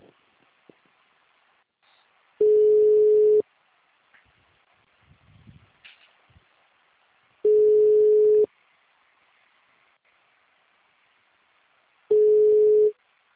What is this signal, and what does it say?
Ringback tone (Europe, 425 Hz).
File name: out- -2020- - - .4